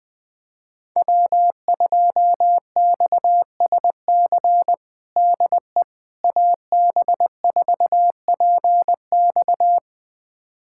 Morse code, heard as a keyed tone, about 20 wpm, 700 Hz. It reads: W2XSC DE AB4PX
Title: USA Callsign DE Callsign - 5 Characters - 20wpm